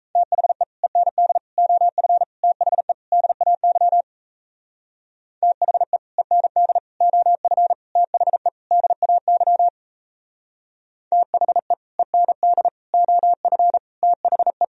Morse code, heard as a keyed tone, about 37 wpm, 700 Hz. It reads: THE END OF THE DAY THE END OF THE DAY THE END OF THE